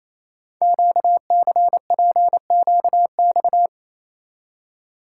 Morse code, heard as a keyed tone, about 27 wpm, 700 Hz. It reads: QCPQX
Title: Code Groups - 5 Letters - 28wpm